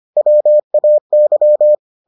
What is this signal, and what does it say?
Morse code, heard as a keyed tone, about 25 wpm, 600 Hz. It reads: WAY